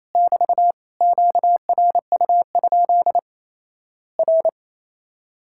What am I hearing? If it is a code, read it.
Morse code, heard as a keyed tone, about 27 wpm, 700 Hz. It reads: <BT> QRU? R
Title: Worked All Europe DX Contest - QTC Exchange - 28wpm